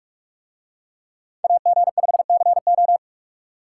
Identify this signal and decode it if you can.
Morse code, heard as a keyed tone, about 44 wpm, 700 Hz. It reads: AG5CO